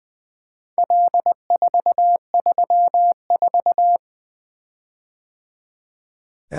Morse code, heard as a keyed tone, about 20 wpm, 700 Hz. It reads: L434
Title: Four Letter-Number Combinations - 20wpm